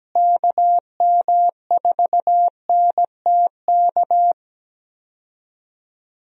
Morse code, heard as a keyed tone, about 17 wpm, 700 Hz. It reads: KM4NTK
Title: USA Call Signs - 6 Characters - Repeated 2x - 17wpm